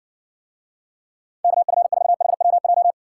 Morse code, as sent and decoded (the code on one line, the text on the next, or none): -.- ..-. ....- .... .-- .---
KF4HWJ